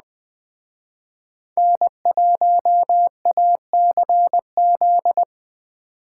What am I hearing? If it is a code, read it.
Morse code, heard as a keyed tone, about 20 wpm, 700 Hz. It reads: N1ACZ